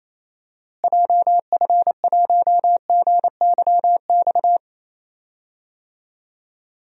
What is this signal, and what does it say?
Morse code, heard as a keyed tone, about 28 wpm, 700 Hz. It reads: JF1GYX
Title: DXCC-100 Call Signs - 6 Characters - Repeated 3x - 28wpm